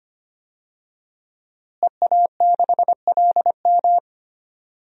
Morse code, heard as a keyed tone, about 25 wpm, 700 Hz. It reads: EA6LM
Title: DXCC-100 Call Signs - 5 Characters - Repeated 3x - 25wpm